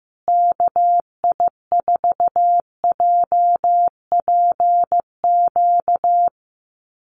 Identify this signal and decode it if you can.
Morse code, heard as a keyed tone, about 15 wpm, 700 Hz. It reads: KI4JPQ
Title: USA Call Signs Repeated 3x 15wpm